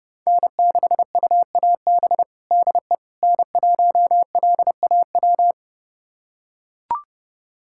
Morse code, heard as a keyed tone, about 30 wpm, 700 Hz. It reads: N6UAB DE N1LAW
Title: USA Callsign DE Callsign - 5 Characters - 30wpm